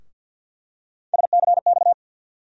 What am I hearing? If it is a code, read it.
Morse code, heard as a keyed tone, about 52 wpm, 700 Hz. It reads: SCX
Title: USA Call Sign Postfixes - 3 Letters 50wpm